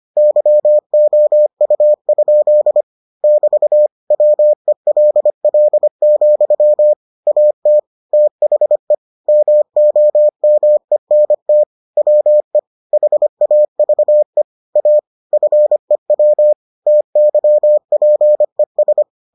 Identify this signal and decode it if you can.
Morse code, heard as a keyed tone, about 25 wpm, 600 Hz. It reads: YOU? <BT> WELL, AT THE MOMENT WE HAVE A FEW TYPES